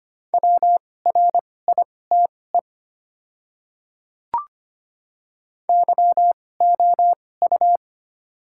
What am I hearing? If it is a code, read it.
Morse code, heard as a keyed tone, about 25 wpm, spaced as for 18 wpm, 690 Hz. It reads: WRITE YOU
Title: 2x Character Spacing - Top 500 Words - 25wpm